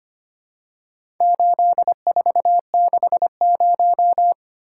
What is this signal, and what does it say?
Morse code, heard as a keyed tone, about 25 wpm, 700 Hz. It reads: 8460
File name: practice / 25wpm / Four Letter-Number Combinations - 25wpm